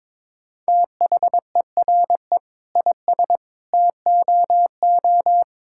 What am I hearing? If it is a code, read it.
Morse code, heard as a keyed tone, about 22 wpm, 700 Hz. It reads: THERE IS TOO
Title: Sentences from Top 1000 Words - Review Part 5 - 22wpm